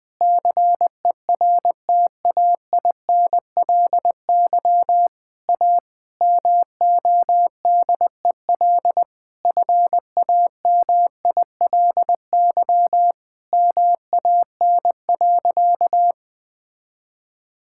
Morse code, heard as a keyed tone, about 20 wpm, 700 Hz. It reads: CERTAINLY A MODEL FAMILY MAN.
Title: Sentences from Top 1000 Words - Review Part 5 - 20wpm